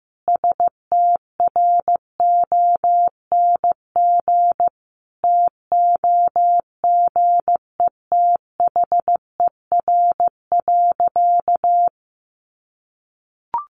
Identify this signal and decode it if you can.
Morse code, heard as a keyed tone, about 15 wpm, 700 Hz. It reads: STRONG TOGETHER.
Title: Sentences from Top 500 Words 15wpm